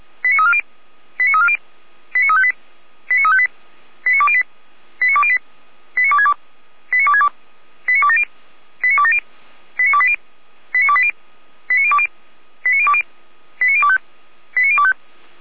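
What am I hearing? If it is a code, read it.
Five-tone selective call (ZVEI), heard as quick runs of five tones, tones of about 70 ms. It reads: 89340 89340 89358 89358 89208 89208 89262 89262 89270 89270 89280 89280 89020 89020 89025 89025
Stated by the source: ZVEI five-tone call received via SDR.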